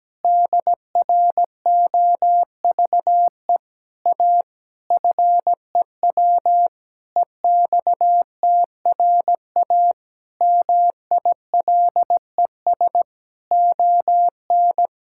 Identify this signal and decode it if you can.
Morse code, heard as a keyed tone, about 17 wpm, 700 Hz. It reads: DROVE A FEW EXTRA MILES ON
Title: Pangram Challenge v2 - 17wpm